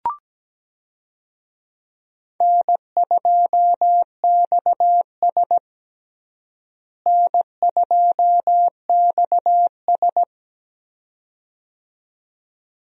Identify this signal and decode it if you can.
Morse code, heard as a keyed tone, about 17 wpm, 700 Hz. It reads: N2XS N2XS
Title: USA Call Signs - 4 Characters - Repeated 2x - 17wpm